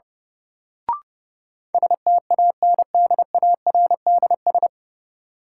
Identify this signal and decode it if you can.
Morse code, heard as a keyed tone, about 30 wpm, 700 Hz. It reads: STANDARDS
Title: Sentences from Top 1000 Words - Review Part 6 - New Word Drill - 30wpm